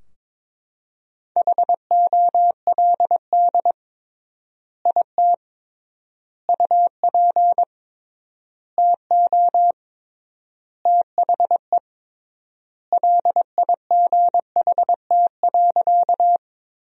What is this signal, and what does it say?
Morse code, heard as a keyed tone, about 22 wpm, 700 Hz. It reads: HOLD IT UP TO THE LIGHT.
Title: 3x Word-Spacing - Sentences from Top 400 Words - 22wpm